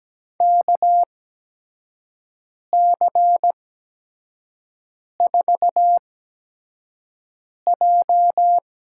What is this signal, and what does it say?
Morse code, heard as a keyed tone, about 17 wpm, spaced as for 5 wpm, 700 Hz. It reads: KC4J
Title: USA Call Signs - 8x Character Spacing - 17wpm